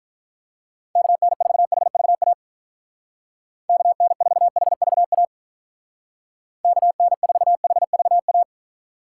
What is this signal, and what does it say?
Morse code, heard as a keyed tone, about 44 wpm, 700 Hz. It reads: KN4HVA KN4HVA KN4HVA